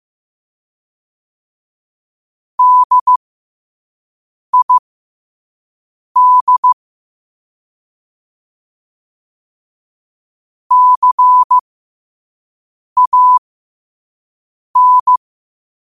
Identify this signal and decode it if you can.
Morse code, heard as a keyed tone, about 15 wpm, spaced as for 5 wpm, 1000 Hz. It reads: DID CAN
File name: qotdfiles / 5wpm